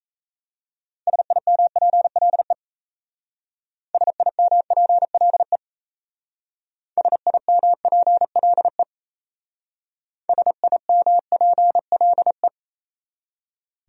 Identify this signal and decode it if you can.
Morse code, heard as a keyed tone, about 36 wpm, 700 Hz. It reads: SIMPLE SIMPLE SIMPLE SIMPLE